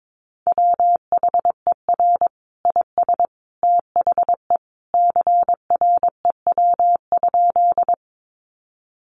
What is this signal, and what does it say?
Morse code, heard as a keyed tone, about 22 wpm, 700 Hz. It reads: WHER IS THE CREW?
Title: Sentences from Top 1500 Words 22wpm